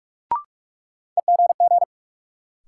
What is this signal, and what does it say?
Morse code, heard as a keyed tone, about 41 wpm, 700 Hz. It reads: EGG
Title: Sentences from Top 1000 Words - Review Part 7 - New Word Drill - 45wpm